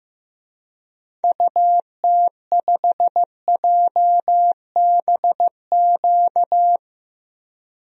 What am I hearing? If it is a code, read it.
Morse code, heard as a keyed tone, about 15 wpm, 700 Hz. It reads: UT5JBQ
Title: DXCC-100 Call Signs - 6 Characters - 15wpm